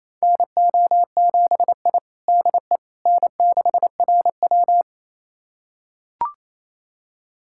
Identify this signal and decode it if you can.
Morse code, heard as a keyed tone, about 28 wpm, 700 Hz. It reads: NO7I DE N6RW
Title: USA Callsign DE Callsign - 4 Characters - 28wpm